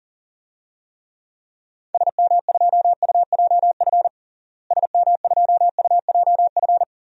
Morse code, heard as a keyed tone, about 40 wpm, 700 Hz. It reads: SM2UJF SM2UJF
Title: DXCC-100 Call Signs - 6 Characters - Repeated 2x - 40wpm